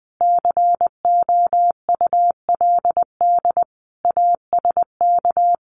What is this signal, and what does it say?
Morse code, heard as a keyed tone, about 20 wpm, 700 Hz. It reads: COULD ASK
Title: Sentences from Top 1400 Words 20wpm